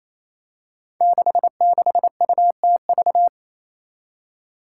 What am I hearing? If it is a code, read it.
Morse code, heard as a keyed tone, about 28 wpm, 700 Hz. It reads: 66UTV
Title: Five Challenging Letter-Number Combinations 28wpm